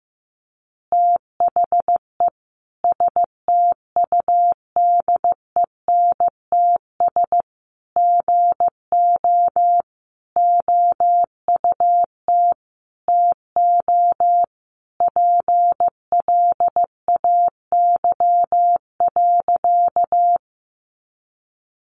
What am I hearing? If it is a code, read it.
Morse code, heard as a keyed tone, about 15 wpm, 700 Hz. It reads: THE STUDENTS GO OUT TO PLAY.
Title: Easy Phrases 15wpm